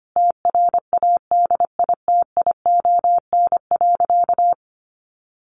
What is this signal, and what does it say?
Morse code, heard as a keyed tone, about 25 wpm, 700 Hz. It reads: TRADITION.
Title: Sentences from Top 1500 Words 25wpm